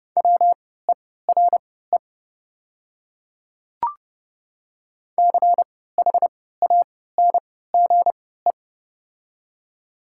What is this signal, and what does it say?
Morse code, heard as a keyed tone, about 30 wpm, spaced as for 17 wpm, 700 Hz. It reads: WERE CHANGE
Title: 3x Character Spacing - Top 500 Words - 30wpm